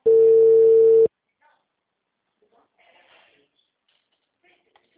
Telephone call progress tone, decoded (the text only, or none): ringback tone